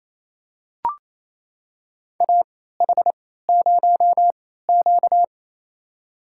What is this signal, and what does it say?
Morse code, heard as a keyed tone, about 28 wpm, spaced as for 16 wpm, 700 Hz. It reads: AH0Q
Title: USA Call Signs - 4 Characters - 3x Character Spacing - 28wpm